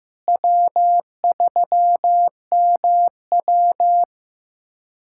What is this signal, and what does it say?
Morse code, heard as a keyed tone, about 16 wpm, 700 Hz. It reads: W3MW
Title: USA Callsign DE Callsign - 4 Characters - 15wpm